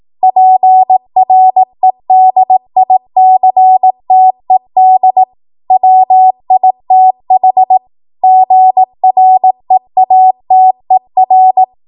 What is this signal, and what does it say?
Morse code, heard as a keyed tone, about 18 wpm, 750 Hz. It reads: PREDICTED WITH GREATER
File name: Morse / 18WPM